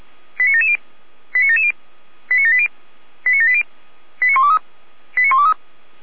Five-tone selective call (ZVEI), heard as quick runs of five tones, tones of about 70 ms. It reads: 89770 89770 89780 89780 89123 89123